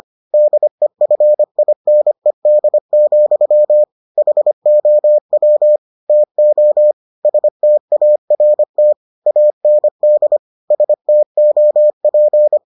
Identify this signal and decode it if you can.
Morse code, heard as a keyed tone, about 25 wpm, 600 Hz. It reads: DEFINED, HOW TO START AND STOP